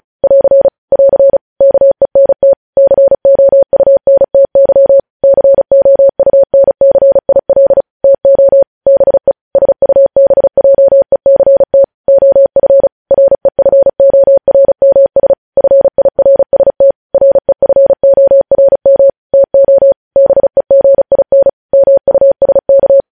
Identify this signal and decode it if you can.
Morse code, heard as a keyed tone, about 36 wpm, 550 Hz. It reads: <AR> <AR> KENT COUNTY COUNCIL TO BE SUBJECT OF REFORMS FIRST REFORM TO BEGIN MUSK
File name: podcast / News headlines-1x-35x35